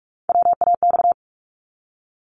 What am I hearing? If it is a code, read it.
Morse code, heard as a keyed tone, about 46 wpm, 700 Hz. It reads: WAX
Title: Multi-letter Phonemes 45wpm